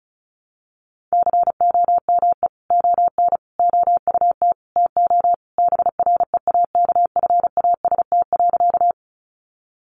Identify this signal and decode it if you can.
Morse code, heard as a keyed tone, about 34 wpm, 700 Hz. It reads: COME ON OUT TO BREAKFAST.
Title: Sentences from Top 1500 Words 35wpm